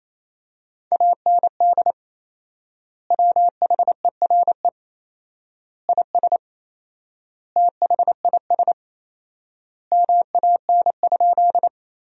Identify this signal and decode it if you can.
Morse code, heard as a keyed tone, about 28 wpm, 700 Hz. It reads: AND WHERE IS THIS MAN?
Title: 4x Word-Spacing - Sentences from Top 200 Words - 28wpm